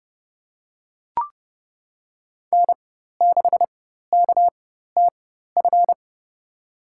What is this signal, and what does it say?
Morse code, heard as a keyed tone, about 30 wpm, spaced as for 14 wpm, 700 Hz. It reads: N6KTF
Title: USA Call Signs - 5 Characters - 4x Character Spacing - 30wpm